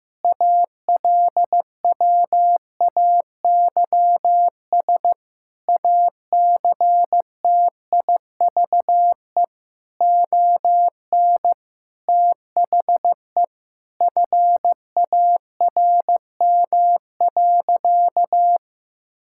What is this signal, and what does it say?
Morse code, heard as a keyed tone, about 15 wpm, 700 Hz. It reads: ALWAYS ACTIVE ON THE FARM.
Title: Sentences from Top 1000 Words - Review Part 6 - 15wpm